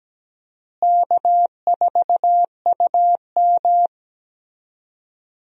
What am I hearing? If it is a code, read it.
Morse code, heard as a keyed tone, about 17 wpm, 700 Hz. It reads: K4UM
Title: USA Call Signs - 4 Characters - Repeated 3x - 17wpm